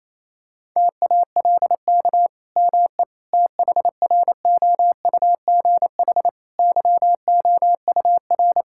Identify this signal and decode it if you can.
Morse code, heard as a keyed tone, about 27 wpm, 700 Hz. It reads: TALK ME THROUGH YOUR
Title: Sentences from Top 1000 Words - Review Part 5 - 28wpm